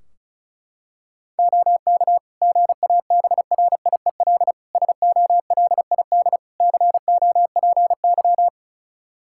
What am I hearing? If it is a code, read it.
Morse code, heard as a keyed tone, about 34 wpm, 700 Hz. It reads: OK GABRIEL SOLID COPY
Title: Practice QSO - Exchange 3 - More Info - 35wpm